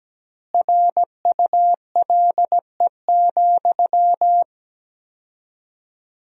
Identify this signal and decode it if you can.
Morse code, heard as a keyed tone, about 17 wpm, 700 Hz. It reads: RULE,
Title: 4x Word-Spacing - Sentences from Top 400 Words - 17wpm